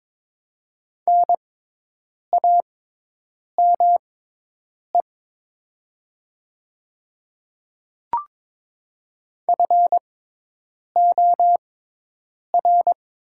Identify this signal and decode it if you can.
Morse code, heard as a keyed tone, about 22 wpm, spaced as for 8 wpm, 700 Hz. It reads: NAME FOR
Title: 6x Character Spacing - Top 300 Words - 22wpm